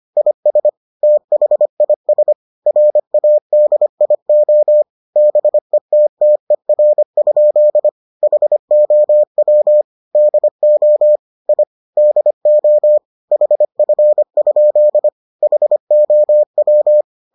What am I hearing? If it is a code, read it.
Morse code, heard as a keyed tone, about 25 wpm, 600 Hz. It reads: IS THIS RADIO BETTER? HOW DO I DO HF? HOW